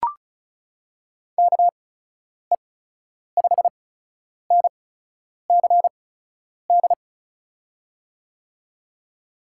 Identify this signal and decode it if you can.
Morse code, heard as a keyed tone, about 36 wpm, spaced as for 10 wpm, 700 Hz. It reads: KE5NCD
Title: USA Call Signs - 6 Characters - 8x Character Spacing - 35wpm